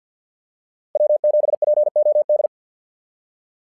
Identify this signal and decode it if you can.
Morse code, heard as a keyed tone, about 50 wpm, 600 Hz. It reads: W7POD
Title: SOTA QSO Practice 50wpm